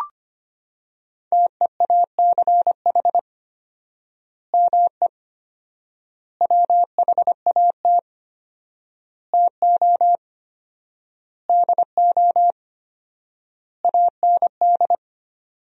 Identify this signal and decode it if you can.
Morse code, heard as a keyed tone, about 25 wpm, 700 Hz. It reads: TEACH ME WHAT TO DO AND
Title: 4x Word-Spacing - Sentences from Top 500 Words - 25wpm